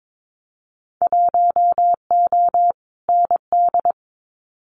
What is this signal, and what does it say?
Morse code, heard as a keyed tone, about 22 wpm, 700 Hz. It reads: 1O ND
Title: Winter Field Day Exchanges 22wpm